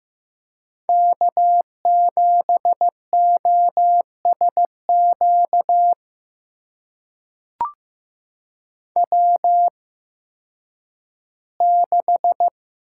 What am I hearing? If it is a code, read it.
Morse code, heard as a keyed tone, about 15 wpm, spaced as for 10 wpm, 700 Hz. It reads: K7OSQ W6
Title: USA Call Signs - 5 Characters - 8x Character Spacing - 15wpm